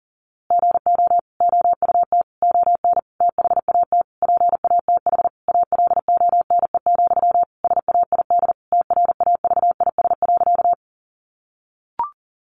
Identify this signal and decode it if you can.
Morse code, heard as a keyed tone, about 40 wpm, 700 Hz. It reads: GO OUT ON THAT PATH ALONE, SAID TRAVIS.
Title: A Sound Of Thunder 40wpm